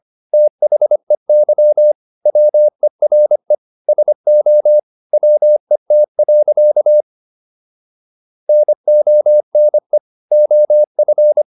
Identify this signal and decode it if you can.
Morse code, heard as a keyed tone, about 25 wpm, 600 Hz. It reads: THEY WERE SO WET. NONE OF